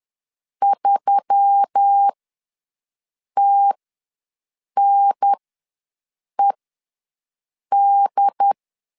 Morse code, heard as a keyed tone, about 10 wpm, spaced as for 6 wpm, 800 Hz. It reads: 3TNED
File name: LEZIONI / 7 GF9Z / 7 GF9 ws 10wpm 4034